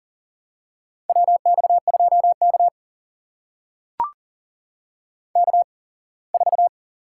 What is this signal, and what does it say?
Morse code, heard as a keyed tone, about 40 wpm, spaced as for 24 wpm, 700 Hz. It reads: WX2K K4
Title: USA Call Signs - 4 Characters - 8x Character Spacing - 40wpm